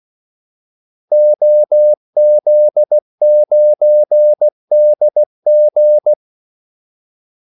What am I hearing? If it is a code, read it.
Morse code, heard as a keyed tone, about 16 wpm, 600 Hz. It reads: OZ9DG
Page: Kaldesignaler 16 wpm